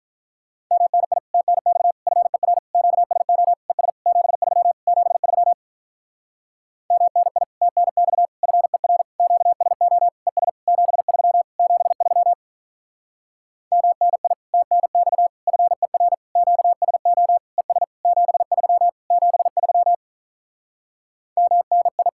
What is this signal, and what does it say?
Morse code, heard as a keyed tone, about 46 wpm, 700 Hz. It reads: MNI TNX FER QSO ES 73 73 MNI TNX FER QSO ES 73 73 MNI TNX FER QSO ES 73 73 MNI